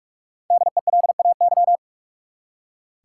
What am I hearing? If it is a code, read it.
Morse code, heard as a keyed tone, about 44 wpm, 700 Hz. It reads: DELAY
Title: Five Letter Words - 45wpm